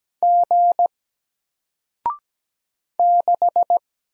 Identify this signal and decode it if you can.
Morse code, heard as a keyed tone, about 17 wpm, 700 Hz. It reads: G 6
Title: Single Letter-Number 17wpm